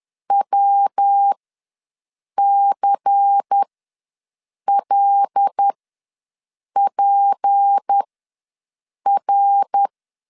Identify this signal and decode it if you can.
Morse code, heard as a keyed tone, about 10 wpm, spaced as for 6 wpm, 800 Hz. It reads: WCLPR
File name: LEZIONI / 8 YJ45 / 8 YJ4 ws 10wpm 4020